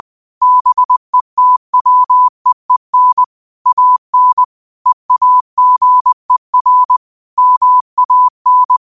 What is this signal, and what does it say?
Morse code, heard as a keyed tone, about 20 wpm, 1000 Hz. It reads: BETWEEN AN EAGER MAN
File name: qotdfiles / 20wpm